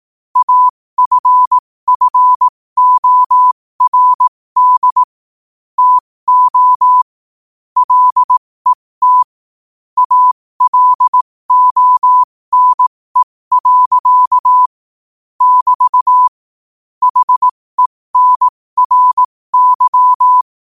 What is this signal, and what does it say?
Morse code, heard as a keyed tone, about 18 wpm, spaced as for 15 wpm, 1000 Hz. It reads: AFFORD TO LET ALONE. <BT> HENRY
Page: New quotes every day in morse code at 15 Words per minute.